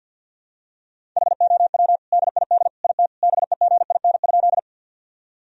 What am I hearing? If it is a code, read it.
Morse code, heard as a keyed tone, about 48 wpm, 700 Hz. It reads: HOW DID IT BEGIN?
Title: Sentences from Top 1000 Words - Review Part 1 - 50wpm